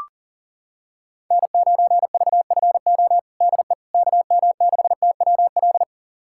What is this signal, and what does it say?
Morse code, heard as a keyed tone, about 40 wpm, 700 Hz. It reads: N9VFO DE KM6TWL
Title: USA Callsign DE Callsign 40wpm